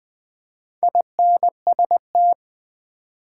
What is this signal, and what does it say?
Morse code, heard as a keyed tone, about 20 wpm, 700 Hz. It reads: INST
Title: Build A Word - 2 Letters At A Time - 8-Letter Words - 20wpm